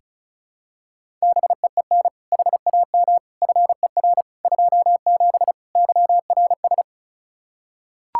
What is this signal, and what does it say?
Morse code, heard as a keyed tone, about 36 wpm, 700 Hz. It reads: BEEN HAM FER 27 YRS
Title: Practice QSO - Exchange 3 - More Info - 35wpm